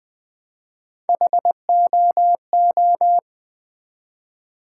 Morse code, heard as a keyed tone, about 20 wpm, 700 Hz. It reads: HOO
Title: Build A Word - 1 Letter At A Time - 4-Letter Words - 20wpm